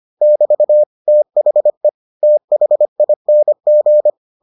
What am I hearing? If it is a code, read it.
Morse code, heard as a keyed tone, about 25 wpm, 600 Hz. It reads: <BT> THE THING